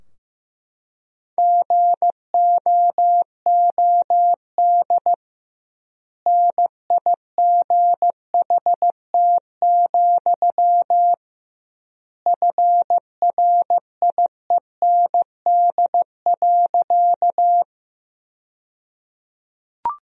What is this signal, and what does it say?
Morse code, heard as a keyed tone, about 15 wpm, 700 Hz. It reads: GOOD NIGHT, FRIEND.
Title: 2x Word-Spacing - Sentences from Top 300 Words - 15wpm